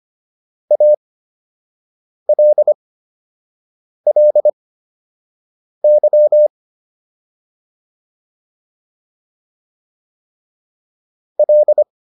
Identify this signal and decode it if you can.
Morse code, heard as a keyed tone, about 25 wpm, spaced as for 6 wpm, 600 Hz. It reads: ALLY L